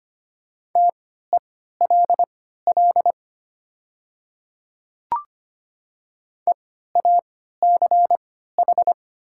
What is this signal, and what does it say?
Morse code, heard as a keyed tone, about 25 wpm, spaced as for 14 wpm, 700 Hz. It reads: TELL EACH
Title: 3x Character Spacing - Top 200 Words - 25wpm